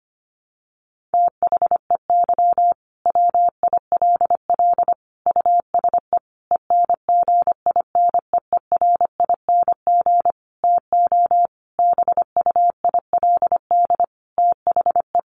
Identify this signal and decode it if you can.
Morse code, heard as a keyed tone, about 25 wpm, 700 Hz. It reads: THEY WILL USE ENGINEERING TO BUILD THE
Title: Sentences from Top 1500 Words 25wpm